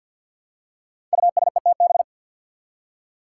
Morse code, heard as a keyed tone, about 52 wpm, 700 Hz. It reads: USETB
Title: Five Challenging Letter-Number Combinations 50wpm